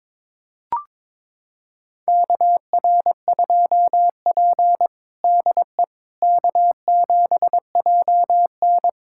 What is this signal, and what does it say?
Morse code, heard as a keyed tone, about 22 wpm, 700 Hz. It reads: KR2P DE K7JN
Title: USA Callsign DE Callsign - 4 Characters - 22wpm